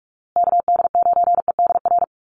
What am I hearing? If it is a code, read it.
Morse code, heard as a keyed tone, about 44 wpm, 700 Hz. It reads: KD9EDR
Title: USA Call Signs Repeated 2x 45wpm